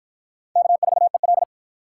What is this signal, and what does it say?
Morse code, heard as a keyed tone, about 52 wpm, 700 Hz. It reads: K4EL